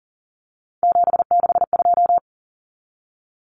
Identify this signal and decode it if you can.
Morse code, heard as a keyed tone, about 40 wpm, 700 Hz. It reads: 762
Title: Common Cut Numbers - 3-Digit 40wpm